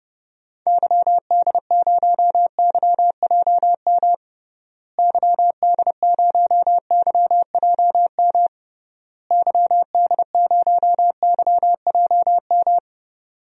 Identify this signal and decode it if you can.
Morse code, heard as a keyed tone, about 30 wpm, 700 Hz. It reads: YD0YJM YD0YJM YD0YJM
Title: DXCC-100 Call Signs - 6 Characters - Repeated 3x - 30wpm